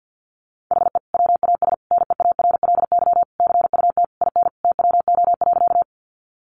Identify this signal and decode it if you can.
Morse code, heard as a keyed tone, about 50 wpm, 700 Hz. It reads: HE WAS NEARLY CUT IN TWO.
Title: Sentences from Top 1300 Words 50wpm